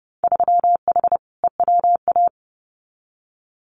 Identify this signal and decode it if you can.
Morse code, heard as a keyed tone, about 30 wpm, 700 Hz. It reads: 3H EWA
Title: Winter Field Day Exchanges Encore 30wpm